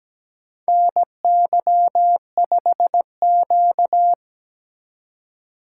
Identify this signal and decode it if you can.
Morse code, heard as a keyed tone, about 17 wpm, 700 Hz. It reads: NY5Q
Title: USA Call Signs - 4 Characters - 4x Character Spacing - 17wpm